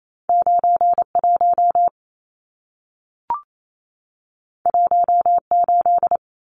Morse code, heard as a keyed tone, about 28 wpm, 700 Hz. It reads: 91 18
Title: Numbers - 2-Digit 28wpm